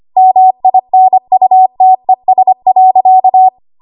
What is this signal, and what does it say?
Morse code, heard as a keyed tone, about 25 wpm, 750 Hz. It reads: MINUTES.
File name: Archive / 25 WPM / 25WPM